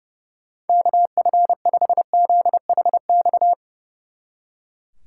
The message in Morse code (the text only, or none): KF5ZHX